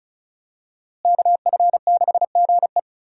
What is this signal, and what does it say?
Morse code, heard as a keyed tone, about 35 wpm, 700 Hz. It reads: KF6GE
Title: USA Call Signs - 5 Characters - Repeated 2x - 35wpm